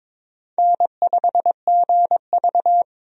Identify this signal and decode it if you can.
Morse code, heard as a keyed tone, about 22 wpm, 700 Hz. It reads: N5GV
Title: USA Call Signs - 4 Characters - 22wpm